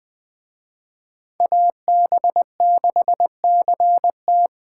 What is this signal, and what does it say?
Morse code, heard as a keyed tone, about 20 wpm, 700 Hz. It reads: AB6CT